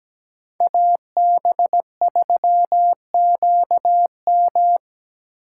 Morse code, heard as a keyed tone, about 17 wpm, 700 Hz. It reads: AB3QM
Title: USA Call Signs - 5 Characters - 4x Character Spacing - 17wpm